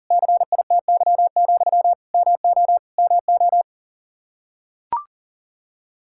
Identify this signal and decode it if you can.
Morse code, heard as a keyed tone, about 42 wpm, 700 Hz. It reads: CITY, MO MO
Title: K1USN SST QSO Practice 40wpm